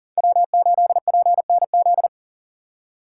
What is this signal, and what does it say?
Morse code, heard as a keyed tone, about 40 wpm, 700 Hz. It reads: W8PNZ
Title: USA Call Signs - 5 Characters - 4x Character Spacing - 40wpm